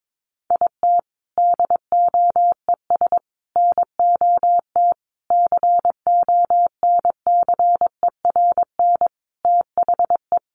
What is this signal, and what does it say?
Morse code, heard as a keyed tone, about 22 wpm, 700 Hz. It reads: IT DOES NOT CONCERN THE
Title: Sentences from Top 1500 Words 22wpm